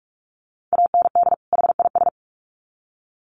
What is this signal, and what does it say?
Morse code, heard as a keyed tone, about 46 wpm, 700 Hz. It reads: AND HIS
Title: Sets of 2 Words 45wpm